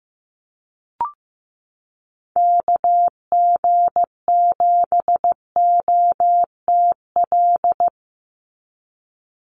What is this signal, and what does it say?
Morse code, heard as a keyed tone, about 15 wpm, 700 Hz. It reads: KG7OTL
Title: USA Call Signs 15wpm